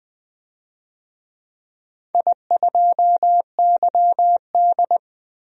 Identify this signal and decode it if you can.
Morse code, heard as a keyed tone, about 20 wpm, 700 Hz. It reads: I2YD
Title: DXCC-100 Call Signs - 4 Characters - 20wpm